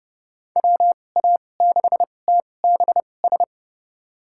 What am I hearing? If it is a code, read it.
Morse code, heard as a keyed tone, about 32 wpm, spaced as for 22 wpm, 700 Hz. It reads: WA6TBS
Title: USA Call Signs - 2x Character Spacing - 30wpm